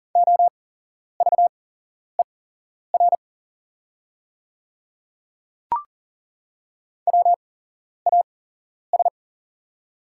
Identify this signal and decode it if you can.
Morse code, heard as a keyed tone, about 40 wpm, spaced as for 11 wpm, 690 Hz. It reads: OVER WAS
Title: 8x Character Spacing - Top 100 Words - 40wpm